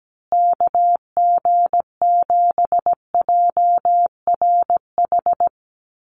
Morse code, heard as a keyed tone, about 17 wpm, 700 Hz. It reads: KG7JRH
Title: USA Call Signs Repeated 2x 17wpm